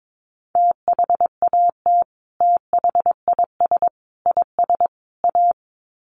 Morse code, heard as a keyed tone, about 22 wpm, 700 Hz. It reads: THAT THIS IS A
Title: Sets of 4 Words 22wpm